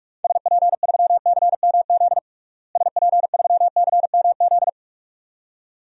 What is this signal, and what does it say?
Morse code, heard as a keyed tone, about 46 wpm, 700 Hz. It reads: SP3CMZ SP3CMZ
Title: DXCC-100 Call Signs - 6 Characters - Repeated 2x - 45wpm